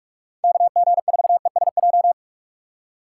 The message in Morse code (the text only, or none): KG4ESJ